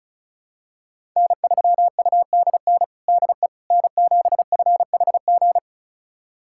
Morse code, heard as a keyed tone, about 34 wpm, 700 Hz. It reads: N3UDN DE N7FHG
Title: USA Callsign DE Callsign - 5 Characters - 35wpm